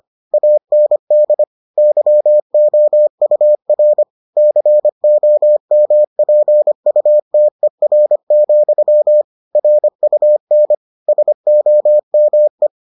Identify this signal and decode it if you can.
Morse code, heard as a keyed tone, about 25 wpm, 600 Hz. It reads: AND YOUR COMPUTER, RUN SOME